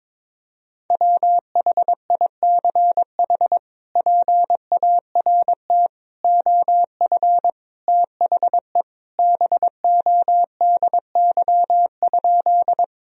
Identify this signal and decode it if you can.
Morse code, heard as a keyed tone, about 22 wpm, 700 Hz. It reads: WHICH PART OF THE BODY?
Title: Sentences from Top 1000 Words - Review Part 2 - 22wpm